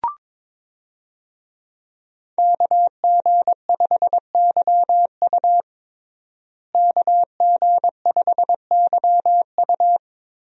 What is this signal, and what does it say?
Morse code, heard as a keyed tone, about 22 wpm, 700 Hz. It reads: KG5YU KG5YU
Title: USA Call Signs - 5 Characters - Repeated 2x - 22wpm